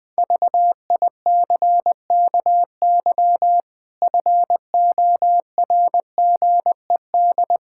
Morse code, heard as a keyed tone, about 20 wpm, 700 Hz. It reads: VICKY FORGED
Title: Pangram Challenge v2 - 20wpm